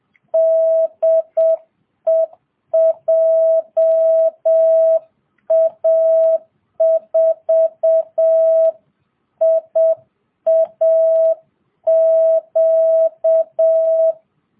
Morse code, heard as a keyed tone, about 7 wpm, 650 Hz. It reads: DE JA4IAQ